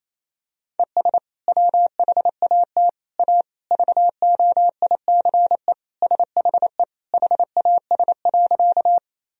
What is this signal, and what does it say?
Morse code, heard as a keyed tone, about 28 wpm, 700 Hz. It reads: ES WHAT A VOICE SHE HAS.
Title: Sentences from Top 500 Words and Abbreviations 28wpm